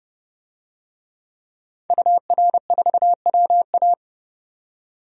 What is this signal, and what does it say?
Morse code, heard as a keyed tone, about 30 wpm, 700 Hz. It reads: UR4WA
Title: DXCC-100 Call Signs - 5 Characters - 30wpm